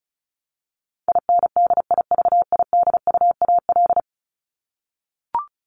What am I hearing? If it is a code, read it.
Morse code, heard as a keyed tone, about 34 wpm, 700 Hz. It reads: INDIVIDUAL
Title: Top 501-750 Words 35wpm